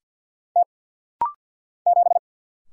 Morse code, heard as a keyed tone, about 50 wpm, 700 Hz. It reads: T 7
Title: Single Letter-Number - Rapid-Fire 50wpm